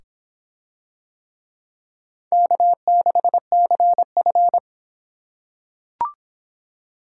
Morse code, heard as a keyed tone, about 26 wpm, 700 Hz. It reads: K6CF